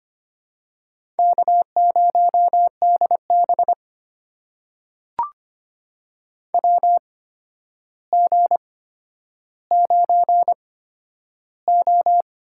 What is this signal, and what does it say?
Morse code, heard as a keyed tone, about 25 wpm, spaced as for 11 wpm, 700 Hz. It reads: K0DB WG9O
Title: USA Call Signs - 4 Characters - 8x Character Spacing - 25wpm